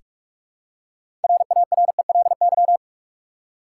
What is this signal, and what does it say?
Morse code, heard as a keyed tone, about 46 wpm, 700 Hz. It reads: RARELY
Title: Six Letter Words - 45wpm